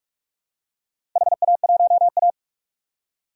Morse code, heard as a keyed tone, about 46 wpm, 700 Hz. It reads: HA1A
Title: DXCC-100 Call Signs - 4 Characters - 45wpm